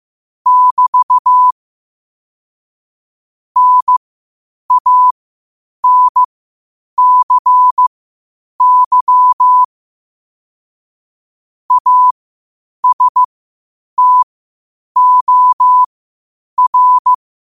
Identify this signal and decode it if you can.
Morse code, heard as a keyed tone, about 15 wpm, spaced as for 8 wpm, 1000 Hz. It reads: <BT> NANCY ASTOR
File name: qotdfiles / 8wpm